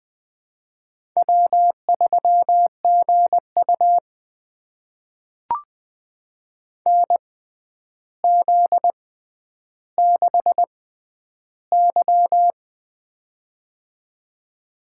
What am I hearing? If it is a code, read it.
Morse code, heard as a keyed tone, about 20 wpm, spaced as for 10 wpm, 700 Hz. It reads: W3GU NZ6Y
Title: USA Call Signs - 4 Characters - 6x Character Spacing - 20wpm